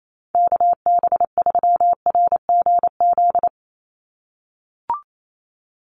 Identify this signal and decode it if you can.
Morse code, heard as a keyed tone, about 28 wpm, 700 Hz. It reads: KB3RGZ
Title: USA Call Signs 28wpm